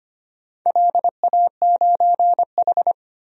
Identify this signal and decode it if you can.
Morse code, heard as a keyed tone, about 25 wpm, 700 Hz. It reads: LA9H
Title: DXCC-100 Call Signs - 4 Characters - 25wpm